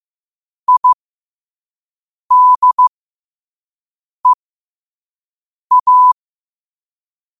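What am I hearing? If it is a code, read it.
Morse code, heard as a keyed tone, about 15 wpm, spaced as for 5 wpm, 1000 Hz. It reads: IDEA
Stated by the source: Morse code at 5 words per minute.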